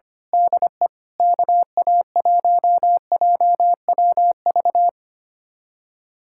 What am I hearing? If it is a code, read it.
Morse code, heard as a keyed tone, about 25 wpm, 700 Hz. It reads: DE KA1JWV
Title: Speed-Racing Common QSO Phrases - 25wpm